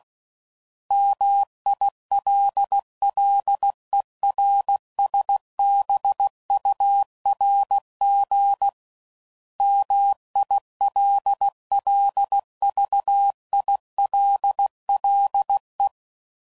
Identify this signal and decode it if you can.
Morse code, heard as a keyed tone, about 16 wpm, 800 Hz. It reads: MILLERSBURG MILLVILLE
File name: CW Code Groups @15WPM